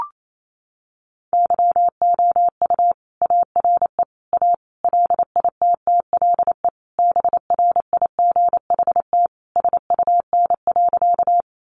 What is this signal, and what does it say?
Morse code, heard as a keyed tone, about 28 wpm, 700 Hz. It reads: YOU ARE A LITTLE BRIGHT SUN.
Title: Sentences from Top 1000 Words 28wpm